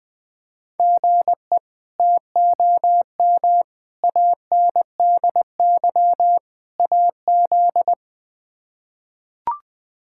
Morse code, heard as a keyed tone, about 20 wpm, 700 Hz. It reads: GE TOM ANDY AZ
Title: K1USN SST QSO Practice - 20wpm